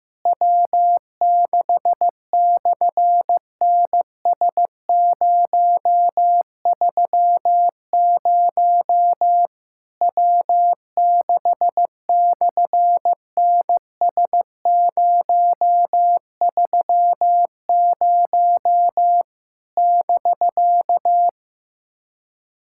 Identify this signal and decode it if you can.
Morse code, heard as a keyed tone, about 15 wpm, 700 Hz. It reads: W6/NS030 W6/NS030 <BK>
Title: SOTA QSO Practice 15wpm